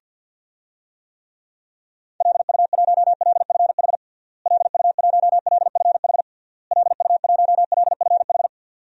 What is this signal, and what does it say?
Morse code, heard as a keyed tone, about 50 wpm, 700 Hz. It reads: LU1LFH LU1LFH LU1LFH
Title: DXCC-100 Call Signs - 6 Characters - Repeated 3x - 50wpm